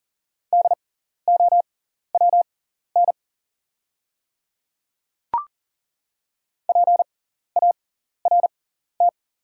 Morse code, heard as a keyed tone, about 40 wpm, spaced as for 14 wpm, 690 Hz. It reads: DOWN PART
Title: 6x Character Spacing - Top 200 Words - 40wpm